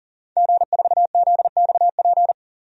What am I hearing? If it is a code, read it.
Morse code, heard as a keyed tone, about 37 wpm, 700 Hz. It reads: G4ZXP
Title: DXCC-100 Call Signs - 5 Characters - 40wpm